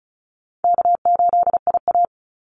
Morse code, heard as a keyed tone, about 34 wpm, 700 Hz. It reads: K8IA
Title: Call Signs Repeated 3x - 35wpm